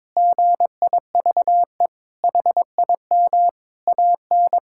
Morse code, heard as a keyed tone, about 22 wpm, 700 Hz. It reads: GIVE HIM AN
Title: Sentences from Top 1000 Words - Review Part 5 - 22wpm